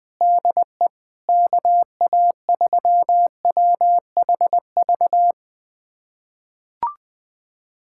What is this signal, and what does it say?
Morse code, heard as a keyed tone, about 20 wpm, 700 Hz. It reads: DE KA3WHV
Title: USA Callsign DE Callsign - 6 Characters - 20wpm